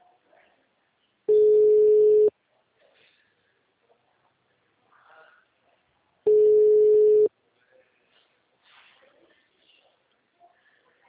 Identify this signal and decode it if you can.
Ringback tone (Europe, 425 Hz).
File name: out- -2035- - - .6